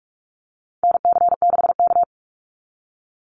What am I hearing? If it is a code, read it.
Morse code, heard as a keyed tone, about 44 wpm, 700 Hz. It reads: NC6K
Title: Call Signs - 45wpm